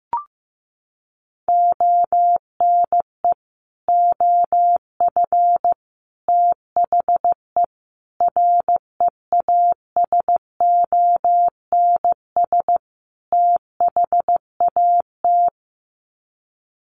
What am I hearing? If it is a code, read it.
Morse code, heard as a keyed tone, about 15 wpm, 700 Hz. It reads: ONE OF THE REASONS THAT
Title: Sets of 5 Words Encore 15wpm